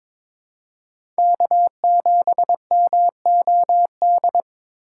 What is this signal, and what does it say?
Morse code, heard as a keyed tone, about 21 wpm, 700 Hz. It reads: K7MOD
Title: USA Call Signs - 5 Characters - 2x Character Spacing - 22wpm